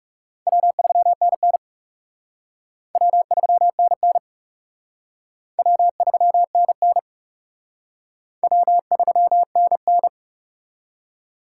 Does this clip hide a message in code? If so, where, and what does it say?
Yes: Morse code from 0.5 to 10.1 s, heard as a keyed tone, about 38 wpm, 700 Hz. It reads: W3NN W3NN W3NN W3NN